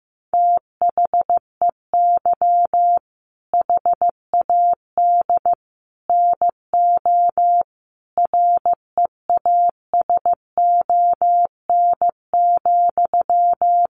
Morse code, heard as a keyed tone, about 14 wpm, 700 Hz. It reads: THEY HAD NO REASON,
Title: Sentences from Top 1500 Words 15wpm